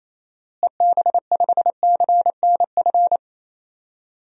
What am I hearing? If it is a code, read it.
Morse code, heard as a keyed tone, about 28 wpm, 700 Hz. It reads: EB5CNF
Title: DXCC-100 Call Signs - 6 Characters - Repeated 2x - 28wpm